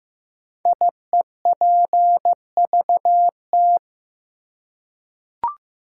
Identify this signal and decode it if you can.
Morse code, heard as a keyed tone, about 15 wpm, 700 Hz. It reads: IEPVT
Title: Code Groups - 5 Letters - 15wpm